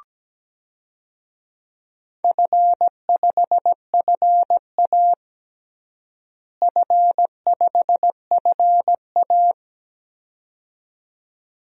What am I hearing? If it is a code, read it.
Morse code, heard as a keyed tone, about 17 wpm, 700 Hz. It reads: F5FA F5FA
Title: DXCC-100 Call Signs - 4 Characters - Repeated 2x - 17wpm